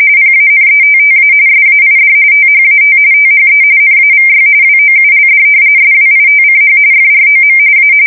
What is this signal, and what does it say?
RTTY, about 45.45 baud. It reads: PANETTA SAID THAT THE NAVY WOULD SWITCH FROM A
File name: fsk170 / usnavy5